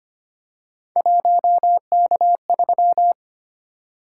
Morse code, heard as a keyed tone, about 25 wpm, 700 Hz. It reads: 1K3
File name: Three Letter-Number Combinations - 25wpm